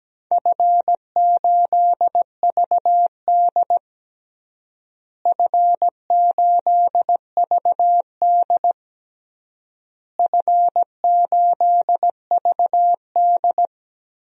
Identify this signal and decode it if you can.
Morse code, heard as a keyed tone, about 17 wpm, 700 Hz. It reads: F8VD F8VD F8VD
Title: DXCC-100 Call Signs - 4 Characters - Repeated 3x - 17wpm